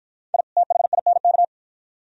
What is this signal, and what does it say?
Morse code, heard as a keyed tone, about 48 wpm, 700 Hz. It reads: I THINK